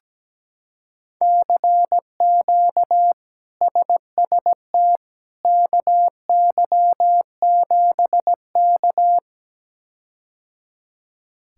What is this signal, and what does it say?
Morse code, heard as a keyed tone, about 17 wpm, 700 Hz. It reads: CQ SST KY7K
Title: K1USN SST QSO Practice - 17wpm